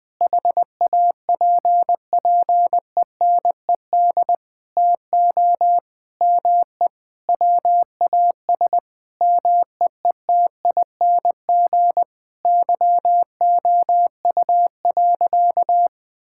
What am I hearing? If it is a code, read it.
Morse code, heard as a keyed tone, about 20 wpm, 700 Hz. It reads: HAPPENED TO ME WAS MEETING YOU.
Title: Sentences with 7-Word Phrases - 20wpm